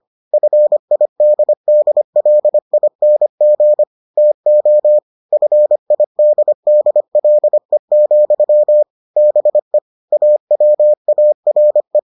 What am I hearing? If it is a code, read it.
Morse code, heard as a keyed tone, about 25 wpm, 600 Hz. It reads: FIDDLING TO FIDDLE, BE AWARE